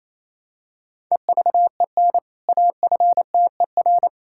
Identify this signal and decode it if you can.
Morse code, heard as a keyed tone, about 28 wpm, 700 Hz. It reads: EVEN AFTER
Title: Sentences from Top 1000 Words - Review Part 5 - 28wpm